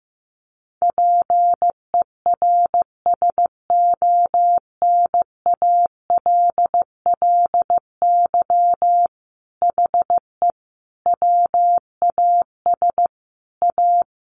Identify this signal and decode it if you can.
Morse code, heard as a keyed tone, about 15 wpm, 700 Hz. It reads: PERSONALLY HE WAS A
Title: Sentences from Top 1400 Words 15wpm